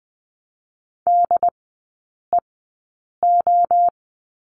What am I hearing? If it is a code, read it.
Morse code, heard as a keyed tone, about 20 wpm, spaced as for 8 wpm, 700 Hz. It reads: DEO
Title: 145 Lesson 24e - Letter y - Three Characters 20wpm